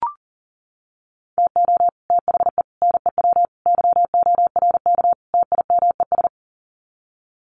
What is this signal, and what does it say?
Morse code, heard as a keyed tone, about 40 wpm, 700 Hz. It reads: TO THE NEW YORK TIMES
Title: Sets of 5 Words Encore2 40wpm